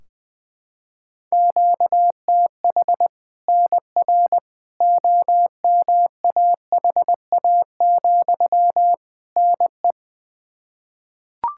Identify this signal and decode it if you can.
Morse code, heard as a keyed tone, about 20 wpm, 700 Hz. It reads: QTH NR OMAHA, NE
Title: Practice QSO - Exchange 1 - Beginning - 20wpm